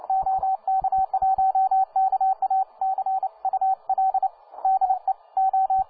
Morse code, heard as a keyed tone, about 29 wpm, 770 Hz, with radio noise. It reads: OK1KA CUL DE O